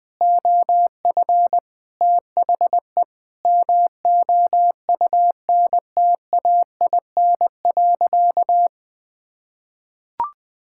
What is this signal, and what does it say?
Morse code, heard as a keyed tone, about 21 wpm, 700 Hz. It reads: OF THE MOUNTAIN.
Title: Sentences with 3-Word Phrases - 20wpm